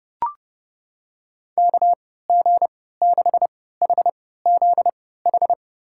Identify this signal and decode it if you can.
Morse code, heard as a keyed tone, about 30 wpm, spaced as for 17 wpm, 700 Hz. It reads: KG6HZH
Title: USA Call Signs - 6 Characters - 3x Character Spacing - 30wpm